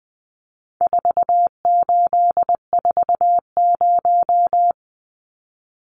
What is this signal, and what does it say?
Morse code, heard as a keyed tone, about 20 wpm, 700 Hz. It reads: 4840
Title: Numbers - 4-Digit 20wpm